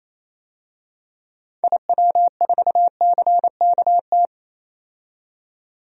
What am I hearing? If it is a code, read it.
Morse code, heard as a keyed tone, about 28 wpm, 700 Hz. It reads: IW4CKT
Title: DXCC-100 Call Signs - 6 Characters - 28wpm